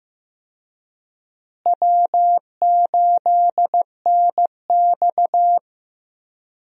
Morse code, heard as a keyed tone, about 15 wpm, 700 Hz. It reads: W8NX
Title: USA Call Signs - 4 Characters - Repeated 3x - 15wpm